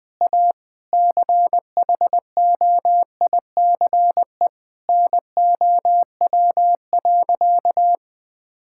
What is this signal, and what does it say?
Morse code, heard as a keyed tone, about 21 wpm, 700 Hz. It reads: A CHOICE NOW.
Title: Sentences with 5-Word Phrases - 20wpm